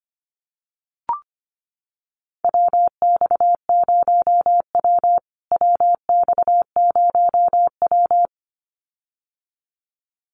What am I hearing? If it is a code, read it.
Morse code, heard as a keyed tone, about 25 wpm, 700 Hz. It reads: WX0W WX0W
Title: USA Call Signs Repeated 2x 25wpm